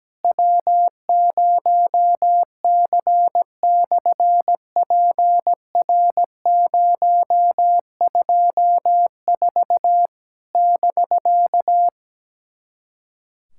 Morse code, heard as a keyed tone, about 17 wpm, 700 Hz. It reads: W0C/PR024 <BK>
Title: SOTA QSO Practice 17wpm